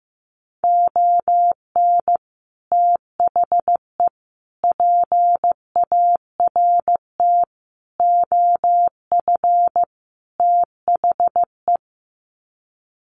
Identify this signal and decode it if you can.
Morse code, heard as a keyed tone, about 15 wpm, 700 Hz. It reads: ON THE PART OF THE
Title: Sets of 5 Words 15wpm